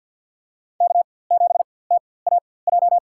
Morse code, heard as a keyed tone, about 53 wpm, spaced as for 24 wpm, 700 Hz. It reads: K7TAJ
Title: USA Call Signs - 5 Characters - 4x Character Spacing - 50wpm